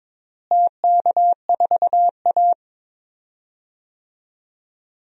Morse code, heard as a keyed tone, about 22 wpm, 700 Hz. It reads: TK4A
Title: DXCC-100 Call Signs - 4 Characters - Repeated 2x - 22wpm